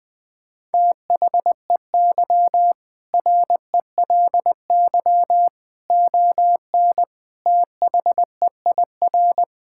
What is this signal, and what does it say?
Morse code, heard as a keyed tone, about 20 wpm, 700 Hz. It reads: THEY RELY ON THEIR
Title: Sentences with 2-Word Phrases - 20wpm